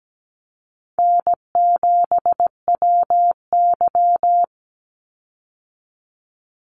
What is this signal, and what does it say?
Morse code, heard as a keyed tone, about 17 wpm, 700 Hz. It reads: N7WY
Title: Easy Call Signs Repeated 3x - 17wpm